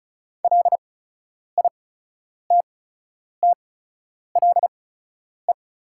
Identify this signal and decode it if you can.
Morse code, heard as a keyed tone, about 36 wpm, spaced as for 10 wpm, 690 Hz. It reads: LITTLE
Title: 8x Character Spacing - Top 300 Words - 35wpm